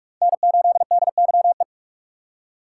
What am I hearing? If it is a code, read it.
Morse code, heard as a keyed tone, about 46 wpm, 700 Hz. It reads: N8DYE
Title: USA Call Signs - 5 Characters - 2x Character Spacing - 45wpm